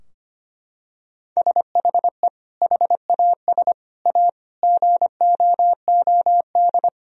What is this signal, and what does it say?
Morse code, heard as a keyed tone, about 25 wpm, 700 Hz. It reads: SHE HAS A GOOD
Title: Sentences from Top 1000 Words - Review Part 4 - 25wpm